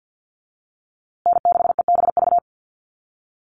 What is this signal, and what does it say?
Morse code, heard as a keyed tone, about 48 wpm, 700 Hz. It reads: N6EBV
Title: USA Call Signs Repeated 3x 50wpm